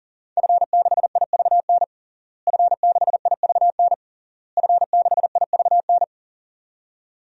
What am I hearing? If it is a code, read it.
Morse code, heard as a keyed tone, about 37 wpm, 700 Hz. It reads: F6IVN F6IVN F6IVN
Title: DXCC-100 Call Signs - 5 Characters - Repeated 3x - 40wpm